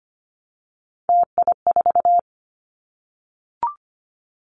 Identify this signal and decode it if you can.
Morse code, heard as a keyed tone, about 25 wpm, 700 Hz. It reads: TI4
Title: Callsign Prefixes for Top 100 DXCC - 25wpm